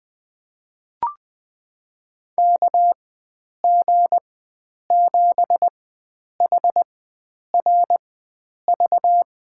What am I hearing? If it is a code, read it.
Morse code, heard as a keyed tone, about 20 wpm, spaced as for 9 wpm, 700 Hz. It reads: KG7HRV
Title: USA Call Signs - 4x Character Spacing - 20wpm